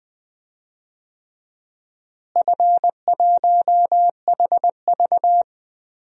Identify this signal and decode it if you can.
Morse code, heard as a keyed tone, about 20 wpm, 700 Hz. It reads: F1HV
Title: DXCC-100 Call Signs - 4 Characters - Repeated 3x - 20wpm